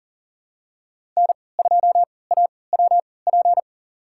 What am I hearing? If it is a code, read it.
Morse code, heard as a keyed tone, about 40 wpm, spaced as for 23 wpm, 700 Hz. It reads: N2AWP
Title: USA Call Signs - 5 Characters - 3x Character Spacing - 40wpm